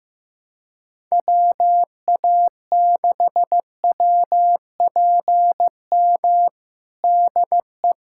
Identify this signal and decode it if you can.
Morse code, heard as a keyed tone, about 15 wpm, 700 Hz. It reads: WA6WPM DE
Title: USA Callsign DE Callsign - 6 Characters - 15wpm